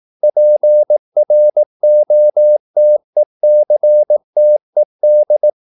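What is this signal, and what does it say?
Morse code, heard as a keyed tone, about 18 wpm, 600 Hz. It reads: PROTECTED
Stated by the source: Morse practice at 18 wpm, English text, from CW med Gnister.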